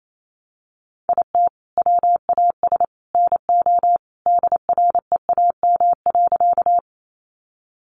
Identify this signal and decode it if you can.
Morse code, heard as a keyed tone, about 28 wpm, 700 Hz. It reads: IT WAS NO DREAM.
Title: Sentences from Top 1500 Words 28wpm